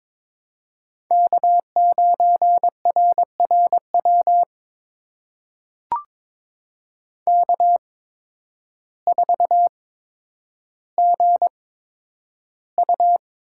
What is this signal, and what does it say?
Morse code, heard as a keyed tone, about 22 wpm, spaced as for 10 wpm, 700 Hz. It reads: K9RRW K4GU
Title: USA Call Signs - 8x Character Spacing - 22wpm